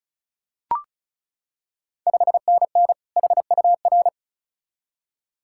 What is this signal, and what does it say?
Morse code, heard as a keyed tone, about 36 wpm, 700 Hz. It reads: 5NN HUR
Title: Ontario QSO Party Exchange 35wpm